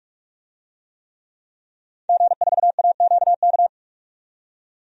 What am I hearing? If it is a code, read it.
Morse code, heard as a keyed tone, about 44 wpm, 700 Hz. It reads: G4AQK
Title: DXCC-100 Call Signs - 5 Characters - 45wpm